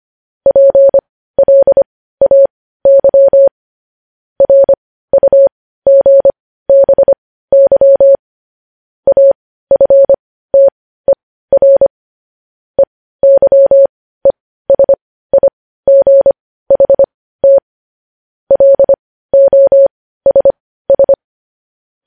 Morse code, heard as a keyed tone, about 25 wpm, spaced as for 15 wpm, 550 Hz. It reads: PLAY RUGBY AFTER EYESIGHT LOSS
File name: podcast / News headlines-1x-25x15